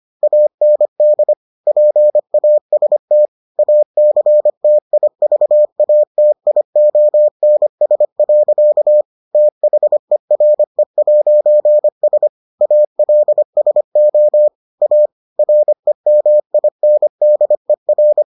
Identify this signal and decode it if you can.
Morse code, heard as a keyed tone, about 25 wpm, 600 Hz. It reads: AND PAST ACTIVATIONS. THERE'S ALSO A REMINDER